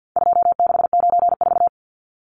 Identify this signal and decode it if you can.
Morse code, heard as a keyed tone, about 48 wpm, 700 Hz. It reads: 2694
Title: Numbers - 4-Digit 50wpm